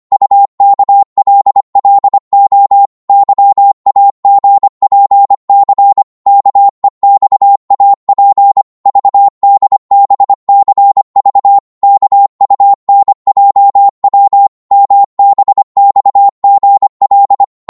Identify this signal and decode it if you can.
Morse code, heard as a keyed tone, about 25 wpm, 800 Hz. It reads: UKLLO YAGPC KEXAP VDBCV KUNJW MBXGL
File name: Morse / ltr-25wpm